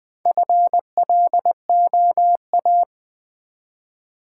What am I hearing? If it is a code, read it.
Morse code, heard as a keyed tone, about 20 wpm, 700 Hz. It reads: FLOA
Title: Build A Word - 1 Letter At A Time - 5-Letter Words - 20wpm